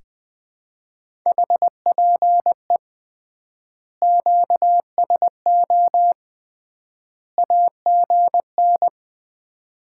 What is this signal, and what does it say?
Morse code, heard as a keyed tone, about 21 wpm, 700 Hz. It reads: HPE QSO AGN
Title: Common QSO Phrases - 3x Word-Spacing - 20wpm